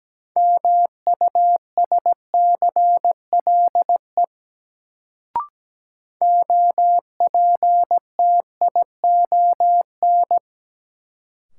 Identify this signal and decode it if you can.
Morse code, heard as a keyed tone, about 17 wpm, 700 Hz. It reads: MUSCLE OPTION
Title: Sentences from Top 1300 Words - New Word Drill - 17wpm